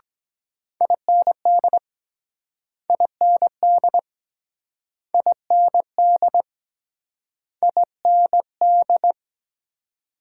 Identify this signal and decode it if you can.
Morse code, heard as a keyed tone, about 23 wpm, 700 Hz. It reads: IND IND IND IND